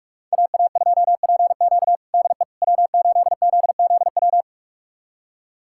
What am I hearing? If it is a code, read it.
Morse code, heard as a keyed tone, about 46 wpm, 700 Hz. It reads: AA2PQ DE W8ZZW
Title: USA Callsign DE Callsign 45wpm